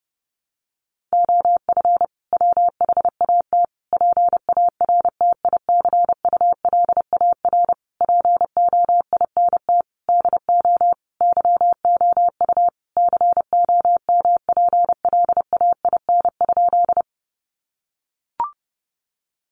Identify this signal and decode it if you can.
Morse code, heard as a keyed tone, about 30 wpm, 700 Hz. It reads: OF WHAT PARTICULAR POINT DO YOU COMPLAIN?
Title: Sentences from Top 1300 Words 30wpm